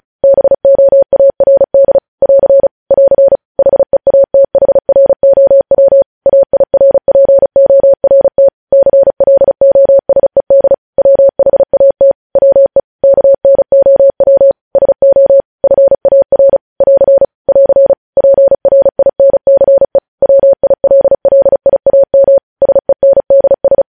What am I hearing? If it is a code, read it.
Morse code, heard as a keyed tone, about 36 wpm, 550 Hz. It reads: BOARD <AR> <AR> HEATHROW AIRPORT CLOSED WHAT WE KNOW SO FAR <AR> <AR> PRINCE WILLIAM SENDS
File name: podcast / News headlines-1x-35x35